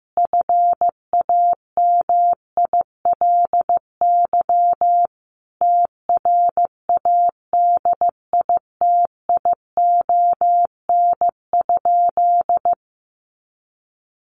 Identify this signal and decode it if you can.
Morse code, heard as a keyed tone, about 15 wpm, 700 Hz. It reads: FAMILY TRADITION?
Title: Sentences from Top 1500 Words 15wpm